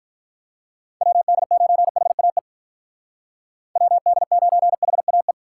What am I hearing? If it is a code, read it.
Morse code, heard as a keyed tone, about 46 wpm, 700 Hz. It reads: WD9HAE WD9HAE